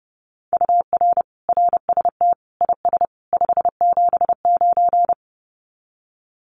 Morse code, heard as a keyed tone, about 30 wpm, 700 Hz. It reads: UR RST IS 579
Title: Common QSO Phrases 30wpm